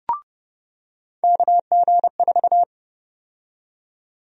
Morse code, heard as a keyed tone, about 30 wpm, 700 Hz. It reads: KG4
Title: USA Call Sign Prefixes - 30wpm